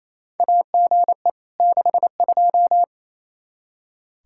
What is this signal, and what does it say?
Morse code, heard as a keyed tone, about 28 wpm, 700 Hz. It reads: AGE 62
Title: K1USN SST QSO Practice 28wpm